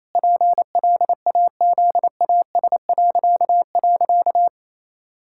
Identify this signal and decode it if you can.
Morse code, heard as a keyed tone, about 28 wpm, 700 Hz. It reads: PLAZAS..
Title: Pangram Challenge v2 - 28wpm